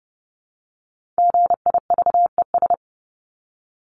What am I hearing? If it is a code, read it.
Morse code, heard as a keyed tone, about 30 wpm, 700 Hz. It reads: GIVES
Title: ES:S Suffix - Top 100 Words with Base Words 30wpm